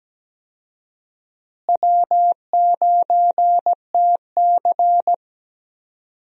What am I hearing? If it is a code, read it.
Morse code, heard as a keyed tone, about 17 wpm, 700 Hz. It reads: W9TC
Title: USA Call Signs - 4 Characters - Repeated 2x - 17wpm